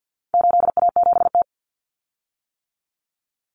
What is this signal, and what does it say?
Morse code, heard as a keyed tone, about 50 wpm, 700 Hz. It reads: 8A7T
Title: Common Cut Numbers - 4-Digit 50wpm